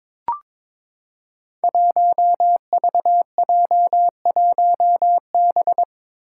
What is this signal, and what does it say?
Morse code, heard as a keyed tone, about 22 wpm, 700 Hz. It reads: 1VJ1B
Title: Five Challenging Letter-Number Combinations 22wpm